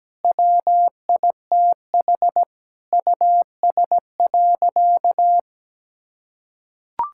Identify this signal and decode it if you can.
Morse code, heard as a keyed tone, about 17 wpm, 700 Hz. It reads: WITH US.
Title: Sentences from Top 1000 Words - Review Part 6 - 17wpm